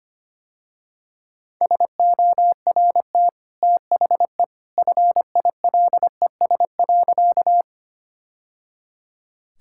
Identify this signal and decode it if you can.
Morse code, heard as a keyed tone, about 25 wpm, 700 Hz. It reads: SORT THE FILES.
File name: Easy Sentences with Hints - 25wpm